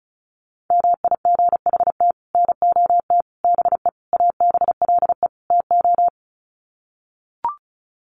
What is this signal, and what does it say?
Morse code, heard as a keyed tone, about 34 wpm, 700 Hz. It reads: MIGHT NOT BE ABLE TO
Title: Sets of 5 Words Encore2 35wpm